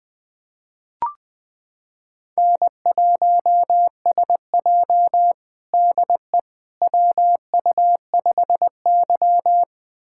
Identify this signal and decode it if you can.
Morse code, heard as a keyed tone, about 21 wpm, 700 Hz. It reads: N1SJ DE WU5Y
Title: USA Callsign DE Callsign - 4 Characters - 20wpm